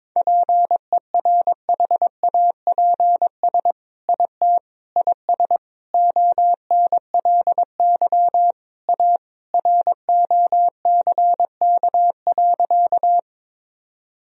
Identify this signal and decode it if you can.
Morse code, heard as a keyed tone, about 22 wpm, 700 Hz. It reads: PERHAPS IT IS ONLY A ROCK.
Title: Sentences from Top 1000 Words - Review Part 2 - 22wpm